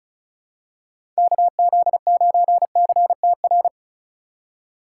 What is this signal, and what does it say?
Morse code, heard as a keyed tone, about 36 wpm, 700 Hz. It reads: KZ9CTR
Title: USA Call Signs - 6 Characters - 2x Character Spacing - 35wpm